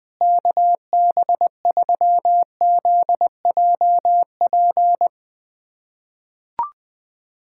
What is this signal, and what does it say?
Morse code, heard as a keyed tone, about 20 wpm, 700 Hz. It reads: KB3ZJP
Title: USA Callsign DE Callsign 20wpm